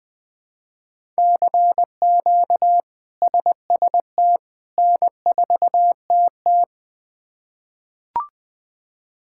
Morse code, heard as a keyed tone, about 21 wpm, 700 Hz. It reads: CQ SST N4TT
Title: K1USN SST QSO Practice - 20wpm